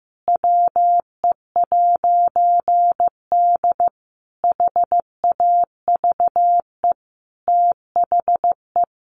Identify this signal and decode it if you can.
Morse code, heard as a keyed tone, about 15 wpm, 700 Hz. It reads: WE'D HAVE THE
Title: A Sound Of Thunder 15wpm